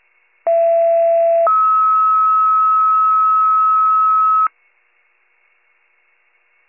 Two-tone page, heard as a one-second tone, then a three-second tone, about 670 then 1290 Hz. It is heard from 0.5 to 4.5 s.
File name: pager / 83 RESCUE OFC / wcalert